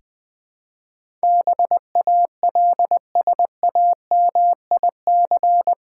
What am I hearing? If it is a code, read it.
Morse code, heard as a keyed tone, about 21 wpm, 690 Hz. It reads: BALSAMIC
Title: Difficult Words with Hints - 20wpm